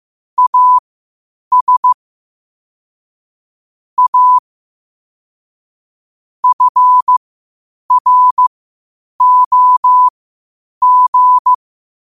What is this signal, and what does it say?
Morse code, heard as a keyed tone, about 15 wpm, spaced as for 8 wpm, 1000 Hz. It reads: AS A FROG
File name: qotdfiles / 8wpm